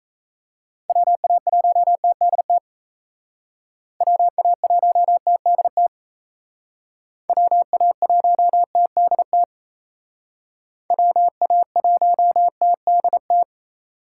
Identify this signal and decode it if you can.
Morse code, heard as a keyed tone, about 36 wpm, 700 Hz. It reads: WA1TDT WA1TDT WA1TDT WA1TDT